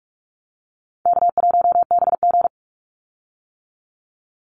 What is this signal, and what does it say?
Morse code, heard as a keyed tone, about 44 wpm, 700 Hz. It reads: K1BG
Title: Easy Call Signs - 45wpm